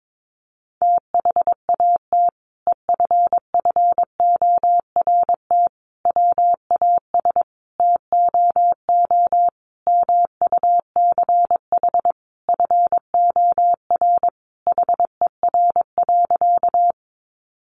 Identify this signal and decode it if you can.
Morse code, heard as a keyed tone, about 22 wpm, 700 Hz. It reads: THAT EFFORT WAS TOO MUCH FOR HER.
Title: Sentences from Top 1200 Words 22wpm